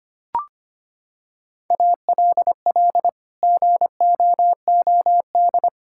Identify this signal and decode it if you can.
Morse code, heard as a keyed tone, about 25 wpm, 700 Hz. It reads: ALL GOOD
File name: Sets of 9 Words - 25wpm